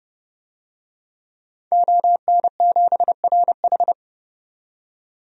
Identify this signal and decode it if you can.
Morse code, heard as a keyed tone, about 29 wpm, 700 Hz. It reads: ON7RH
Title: DXCC-100 Call Signs - 5 Characters - 30wpm